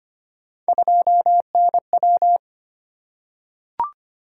Morse code, heard as a keyed tone, about 25 wpm, 700 Hz. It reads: 2NW
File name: Three Letter-Number Combinations - 25wpm